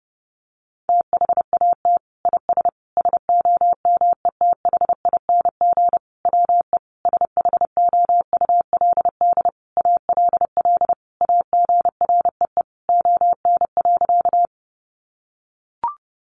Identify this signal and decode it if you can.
Morse code, heard as a keyed tone, about 30 wpm, 700 Hz. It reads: THAT IS SOMETHING WE SHOULD ALL AGREE ON.
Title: Sentences from Top 1100 Words 30wpm